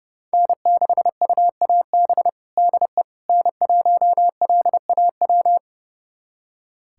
Morse code, heard as a keyed tone, about 30 wpm, 700 Hz. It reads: N6UAB DE N1LAW
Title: USA Callsign DE Callsign - 5 Characters - 30wpm